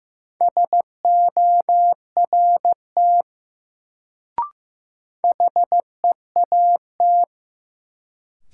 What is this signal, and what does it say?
Morse code, heard as a keyed tone, about 15 wpm, 700 Hz. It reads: SORT HEAT
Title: Sentences from Top 1000 Words - Review Part 3 - New Word Drill - 15wpm